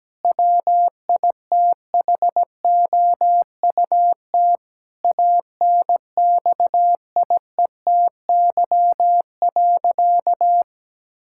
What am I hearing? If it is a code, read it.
Morse code, heard as a keyed tone, about 17 wpm, 700 Hz. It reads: WITHOUT ANXIETY.
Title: Sentences from Top 1600 Words 17wpm